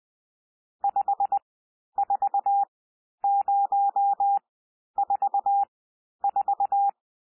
Morse code, heard as a keyed tone, about 20 wpm, spaced as for 11 wpm, 800 Hz. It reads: 54044
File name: cwmp3 / N1 SOUND NUMBERS 20 wpm